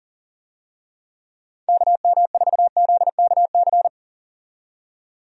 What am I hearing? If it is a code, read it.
Morse code, heard as a keyed tone, about 40 wpm, 700 Hz. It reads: KM4ZKC
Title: USA Call Signs - 6 Characters - 40wpm